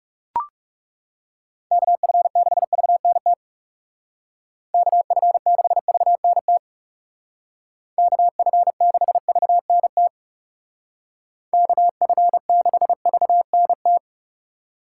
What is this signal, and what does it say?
Morse code, heard as a keyed tone, about 35 wpm, 700 Hz. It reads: KF6VNT KF6VNT KF6VNT KF6VNT
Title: Speed Racing - USA 2x3 Call Signs 30wpm